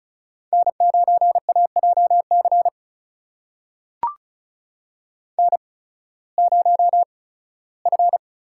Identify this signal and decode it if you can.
Morse code, heard as a keyed tone, about 36 wpm, spaced as for 19 wpm, 700 Hz. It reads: N9AJC N0F
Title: USA Call Signs - 5 Characters - 8x Character Spacing - 35wpm